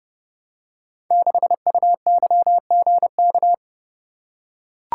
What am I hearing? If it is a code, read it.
Morse code, heard as a keyed tone, about 30 wpm, 700 Hz. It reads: 6UYGK
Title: Code Groups - 5 Letter:Numbers - 30wpm